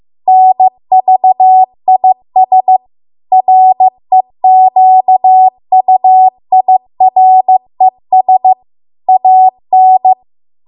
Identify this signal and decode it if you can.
Morse code, heard as a keyed tone, about 15 wpm, 750 Hz. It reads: NVIS REQUIRES AN